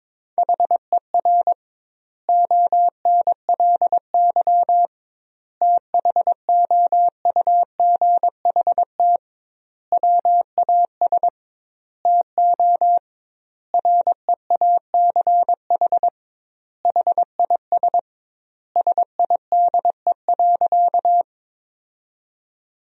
Morse code, heard as a keyed tone, about 22 wpm, 700 Hz. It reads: HER ONLY THOUGHT WAS TO REACH HIS SIDE.
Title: 2x Word-Spacing - Sentences from Top 400 Words - 22wpm